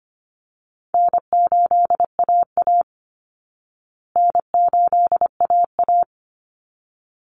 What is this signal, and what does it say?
Morse code, heard as a keyed tone, about 25 wpm, 700 Hz. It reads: N8AA N8AA
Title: Call Signs Repeated 3x - 25wpm